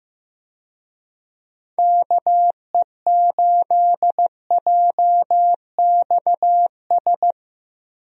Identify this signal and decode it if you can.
Morse code, heard as a keyed tone, about 15 wpm, 700 Hz. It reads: KE8JXS
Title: USA Call Signs - 6 Characters - 15wpm